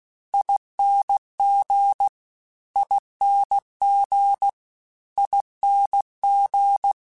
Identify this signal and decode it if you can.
Morse code, heard as a keyed tone, about 16 wpm, 800 Hz. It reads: ING ING ING
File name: Special INGs / Special INGs-mp3s / Sngl ing @15